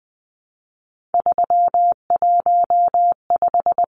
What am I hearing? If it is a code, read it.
Morse code, heard as a keyed tone, about 20 wpm, 700 Hz. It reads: 315
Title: Numbers - 3-Digit 20wpm